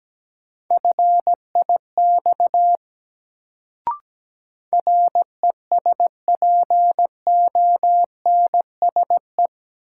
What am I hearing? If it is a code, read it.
Morse code, heard as a keyed tone, about 17 wpm, 700 Hz. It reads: FIX RESPONSE
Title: Sentences from Top 1400 Words - New Word Drill - 17wpm